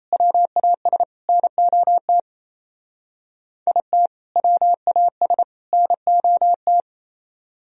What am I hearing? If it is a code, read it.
Morse code, heard as a keyed tone, about 36 wpm, 700 Hz. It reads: WAS NOT IT WAS NOT